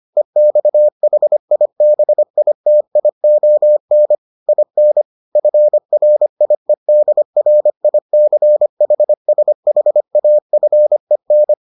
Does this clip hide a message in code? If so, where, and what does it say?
Yes: Morse code from 0.2 to 11.5 s, heard as a keyed tone, about 25 wpm, 600 Hz. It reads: EXHIBITION IN FRIEDRICHSHAFEN